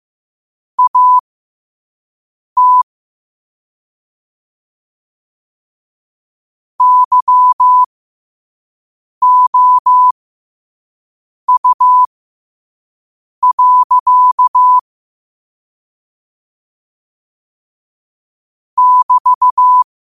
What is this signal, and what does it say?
Morse code, heard as a keyed tone, about 15 wpm, spaced as for 5 wpm, 1000 Hz. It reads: AT YOU. <BT>
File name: qotdfiles / 5wpm